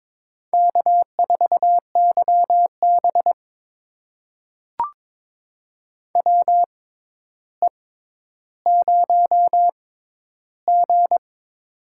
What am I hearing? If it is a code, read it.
Morse code, heard as a keyed tone, about 22 wpm, spaced as for 11 wpm, 700 Hz. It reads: K4YB WE0G
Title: USA Call Signs - 4 Characters - 6x Character Spacing - 22wpm